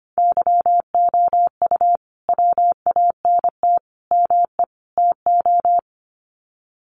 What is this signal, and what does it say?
Morse code, heard as a keyed tone, about 25 wpm, 700 Hz. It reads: YOU WANT ME TO
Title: Sets of 4 Words 25wpm